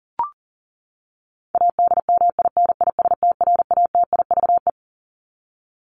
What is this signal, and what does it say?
Morse code, heard as a keyed tone, about 40 wpm, 700 Hz. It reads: ADMINISTRATIVE
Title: Top 1501-1750 Words 40wpm